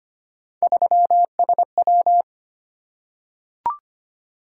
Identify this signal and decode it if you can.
Morse code, heard as a keyed tone, about 25 wpm, 700 Hz. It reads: 3SW
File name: Three Letter-Number Combinations - 25wpm